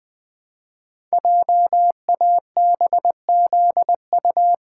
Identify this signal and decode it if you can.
Morse code, heard as a keyed tone, about 20 wpm, 700 Hz. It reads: JABZU
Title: Code Groups - 5 Letters - 20wpm